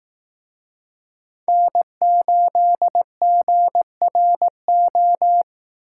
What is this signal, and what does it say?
Morse code, heard as a keyed tone, about 19 wpm, 700 Hz. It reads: N8GRO